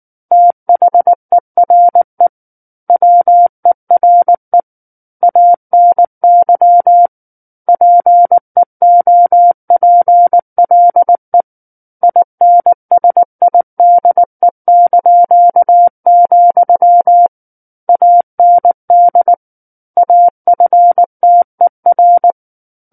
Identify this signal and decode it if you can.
Morse code, heard as a keyed tone, about 19 wpm, 700 Hz. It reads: THERE WERE ANY PEOPLE INSIDE), AND AFTER
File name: Book One / 10-Chapter 10 - 19 WPM